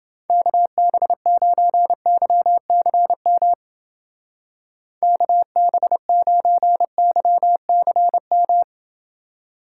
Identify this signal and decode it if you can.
Morse code, heard as a keyed tone, about 28 wpm, 700 Hz. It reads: KB9YCM KB9YCM